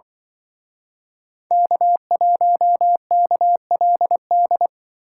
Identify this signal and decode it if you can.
Morse code, heard as a keyed tone, about 24 wpm, 700 Hz. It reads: K1KLD
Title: Speed Racing - USA 1x3 Call Signs 20wpm